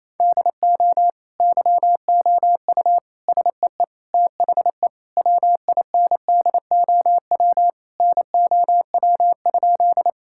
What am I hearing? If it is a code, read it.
Morse code, heard as a keyed tone, about 28 wpm, 700 Hz. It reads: DO YOU SEE THE WINDOW NOW?
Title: Sentences from Top 1600 Words 28wpm